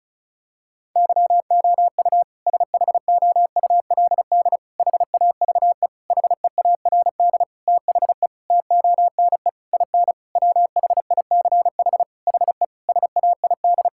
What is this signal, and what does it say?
Morse code, heard as a keyed tone, about 34 wpm, 700 Hz. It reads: YOU SHOULD HAVE HEARD THE TONE IN WHICH HE SAID
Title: Sentences from Top 1600 Words 35wpm